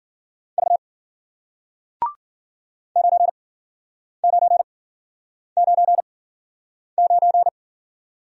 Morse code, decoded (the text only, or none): V 9999